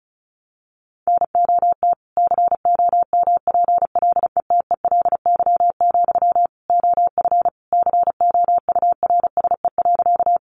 Morse code, heard as a keyed tone, about 34 wpm, 700 Hz. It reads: NOT COMPLETELY, OF COURSE.
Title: Sentences from Top 1100 Words 35wpm